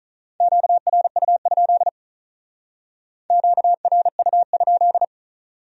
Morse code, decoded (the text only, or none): QRU? QRU?